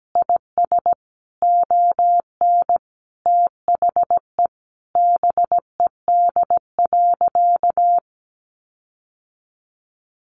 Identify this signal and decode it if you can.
Morse code, heard as a keyed tone, about 17 wpm, 700 Hz. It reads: IS ON THE BED.
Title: Sentences from Top 1400 Words 17wpm